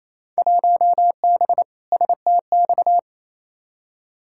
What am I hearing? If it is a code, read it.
Morse code, heard as a keyed tone, about 28 wpm, 700 Hz. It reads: 1B STX
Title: Field Day Exchanges Encore - 28wpm